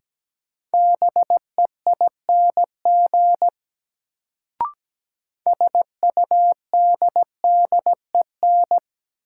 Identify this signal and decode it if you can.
Morse code, heard as a keyed tone, about 17 wpm, 700 Hz. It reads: BEING SUDDEN
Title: Sets of 4 Words - New Word Drill - 17wpm